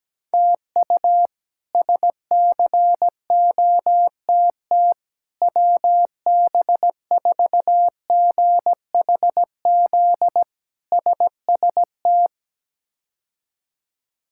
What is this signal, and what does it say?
Morse code, heard as a keyed tone, about 17 wpm, 700 Hz. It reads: TU SCOTT WB4GHZ SST
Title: K1USN SST QSO Practice - 17wpm